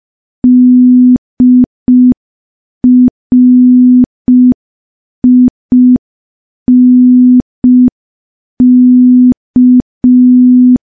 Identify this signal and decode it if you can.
Morse code, heard as a keyed tone, about 5 wpm, 250 Hz. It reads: DRINK
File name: morse / morse-training-05WPM